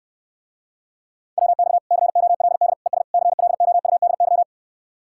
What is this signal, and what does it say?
Morse code, heard as a keyed tone, about 69 wpm, 700 Hz. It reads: WX COLD ES CLOUDY